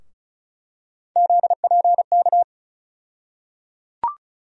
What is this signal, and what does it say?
Morse code, heard as a keyed tone, about 36 wpm, 700 Hz. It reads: ZPK
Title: USA Call Sign Postfixes - 3 Letters 35wpm